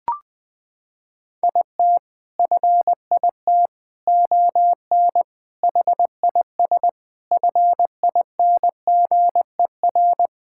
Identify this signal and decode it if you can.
Morse code, heard as a keyed tone, about 20 wpm, 700 Hz. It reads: IT FIT ON HIS FINGER
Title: Sentences from Top 1600 Words 20wpm